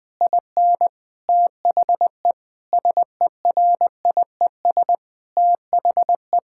Morse code, heard as a keyed tone, about 19 wpm, 700 Hz. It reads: IN THE SERIES THE
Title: Sentences with 6-Word Phrases - 20wpm